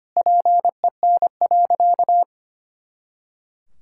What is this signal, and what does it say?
Morse code, heard as a keyed tone, about 25 wpm, 700 Hz. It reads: PEN.
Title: Pangram Challenge v2 - 25wpm